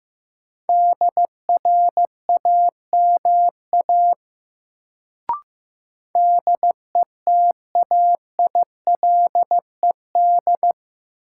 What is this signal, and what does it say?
Morse code, heard as a keyed tone, about 15 wpm, 700 Hz. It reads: DRAMA DETAILED
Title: Sentences from Top 1500 Words - New Word Drill - 15wpm